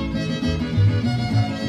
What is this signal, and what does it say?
There is no code.